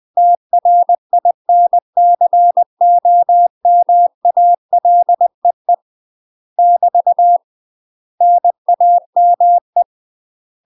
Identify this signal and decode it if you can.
Morse code, heard as a keyed tone, about 20 wpm, 700 Hz. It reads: TRINCOMALEE <BT> NAME
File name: UK MASTER QSOs TRAINING / 20WPM / UK MASTER QSOs 0073